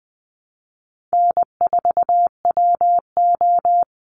Morse code, heard as a keyed tone, about 20 wpm, 700 Hz. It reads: N4WO
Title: Call Signs Repeated 3x - 20wpm